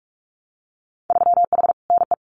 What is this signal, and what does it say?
Morse code, heard as a keyed tone, about 46 wpm, 700 Hz. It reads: 3H NE
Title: Winter Field Day Exchanges Encore 45wpm